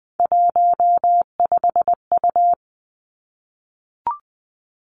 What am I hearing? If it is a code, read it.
Morse code, heard as a keyed tone, about 20 wpm, 700 Hz. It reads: 15U
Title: Three Challenging Letter-Number Combinations 20wpm